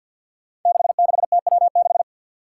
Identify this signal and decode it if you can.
Morse code, heard as a keyed tone, about 53 wpm, 700 Hz. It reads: 66TW6
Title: Five Challenging Letter-Number Combinations 50wpm